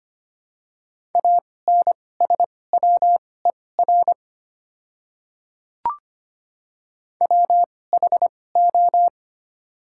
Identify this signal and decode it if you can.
Morse code, heard as a keyed tone, about 25 wpm, spaced as for 18 wpm, 700 Hz. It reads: ANSWER WHO
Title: 2x Character Spacing - Top 300 Words - 25wpm